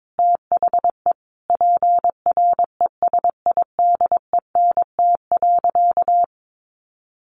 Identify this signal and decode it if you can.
Morse code, heard as a keyed tone, about 22 wpm, 700 Hz. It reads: THE PRESIDENT.
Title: Sentences from Top 1500 Words 22wpm